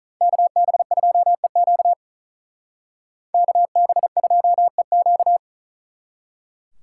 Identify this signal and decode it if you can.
Morse code, heard as a keyed tone, about 45 wpm, 700 Hz. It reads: KB2EQ KB2EQ